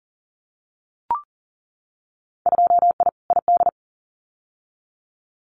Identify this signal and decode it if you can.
Morse code, heard as a keyed tone, about 40 wpm, 700 Hz. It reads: 2I ID
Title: Winter Field Day Exchanges Encore 40wpm